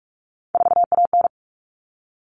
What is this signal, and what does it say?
Morse code, heard as a keyed tone, about 44 wpm, 700 Hz. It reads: 4AN
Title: All Cut Numbers - 3-Digit 45wpm